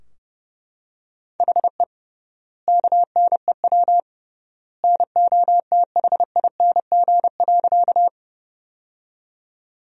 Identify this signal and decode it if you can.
Morse code, heard as a keyed tone, about 30 wpm, 700 Hz. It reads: HE KNEW NOTHING.
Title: 3x Word-Spacing - Sentences from Top 500 Words - 30wpm